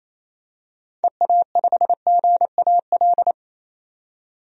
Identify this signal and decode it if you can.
Morse code, heard as a keyed tone, about 28 wpm, 700 Hz. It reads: EA5GAL
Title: DXCC-100 Call Signs - 6 Characters - 28wpm